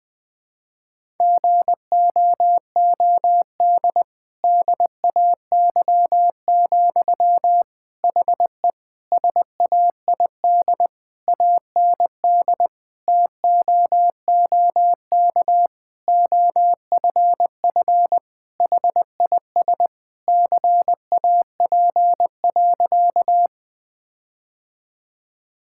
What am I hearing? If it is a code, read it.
Morse code, heard as a keyed tone, about 20 wpm, 700 Hz. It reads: GOOD DAY, HE SAID AND TOOK OFF HIS CAP.
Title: Sentences from Top 1600 Words 20wpm